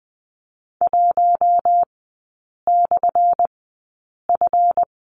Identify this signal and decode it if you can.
Morse code, heard as a keyed tone, about 20 wpm, spaced as for 8 wpm, 700 Hz. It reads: 1/F
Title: 178 Lesson 29e - Slash - Three Characters 20wpm